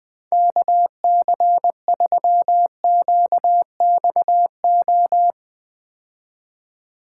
Morse code, heard as a keyed tone, about 20 wpm, 700 Hz. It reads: KC3QXO
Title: 2023 POTA Activator Call Signs - 20wpm